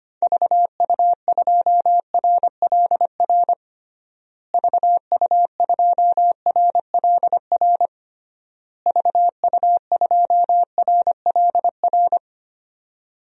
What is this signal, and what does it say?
Morse code, heard as a keyed tone, about 25 wpm, 700 Hz. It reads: VU2RLR VU2RLR VU2RLR
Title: DXCC-100 Call Signs - 6 Characters - Repeated 3x - 25wpm